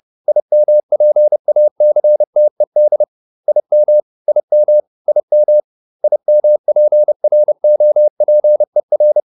Morse code, heard as a keyed tone, about 30 wpm, 600 Hz. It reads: IMPACTED IM IM IM IMPROPER
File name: IM8 30